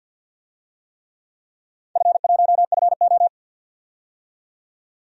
Morse code, heard as a keyed tone, about 53 wpm, 700 Hz. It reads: F1FO
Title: DXCC-100 Call Signs - 4 Characters - 50wpm